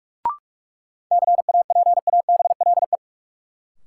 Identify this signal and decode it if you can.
Morse code, heard as a keyed tone, about 42 wpm, 700 Hz. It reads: CAPABLE
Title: Sentences from Top 1200 Words - New Word Drill Alt - 45wpm